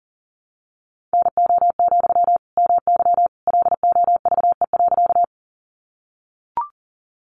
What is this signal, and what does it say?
Morse code, heard as a keyed tone, about 40 wpm, 700 Hz. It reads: NO, MY LOVE.
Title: Sentences from Top 400 Words 40wpm